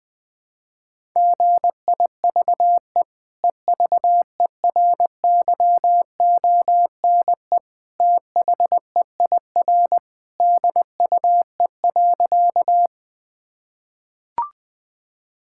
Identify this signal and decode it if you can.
Morse code, heard as a keyed tone, about 20 wpm, 700 Hz. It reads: GIVE EVERYONE THEIR DUE.
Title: Sentences from Top 1000 Words - Review Part 6 - 20wpm